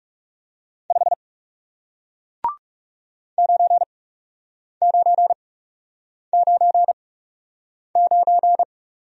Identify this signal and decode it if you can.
Morse code, heard as a keyed tone, about 40 wpm, spaced as for 8 wpm, 700 Hz. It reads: 5 9999